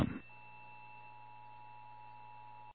Emergency Alert System (SAME) tones (the 853 and 960 Hz attention tone), with noise. It is heard from 0.5 to 2.7 s.
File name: eas / KWBE / KWBE